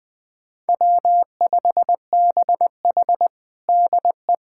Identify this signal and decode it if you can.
Morse code, heard as a keyed tone, about 20 wpm, 700 Hz. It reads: W5BH DE
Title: USA Callsign DE Callsign - 4 Characters - 20wpm